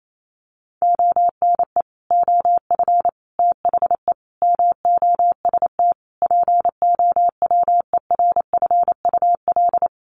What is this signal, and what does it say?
Morse code, heard as a keyed tone, about 28 wpm, 700 Hz. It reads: ONE OF THE MOST POWERFUL
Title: Sets of 5 Words Encore 28wpm